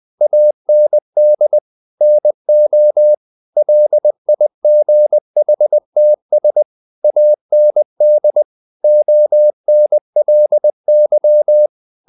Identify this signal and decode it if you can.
Morse code, heard as a keyed tone, about 20 wpm, 600 Hz. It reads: AND NO LIGHTS AND ONLY
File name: the old man and the sea part 2 20wpm